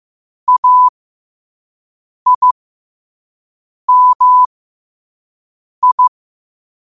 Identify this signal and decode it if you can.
Morse code, heard as a keyed tone, about 15 wpm, spaced as for 5 wpm, 1000 Hz. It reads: AIMI
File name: qotdfiles / 5wpm